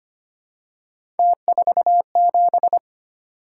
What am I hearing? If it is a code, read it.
Morse code, heard as a keyed tone, about 25 wpm, 700 Hz. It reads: T47
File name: Three Letter-Number Combinations - 25wpm